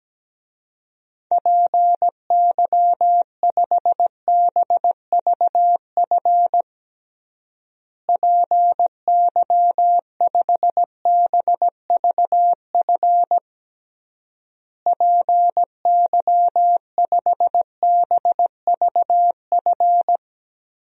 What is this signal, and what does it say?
Morse code, heard as a keyed tone, about 17 wpm, 700 Hz. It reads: PY5BVF PY5BVF PY5BVF
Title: DXCC-100 Call Signs - 6 Characters - Repeated 3x - 17wpm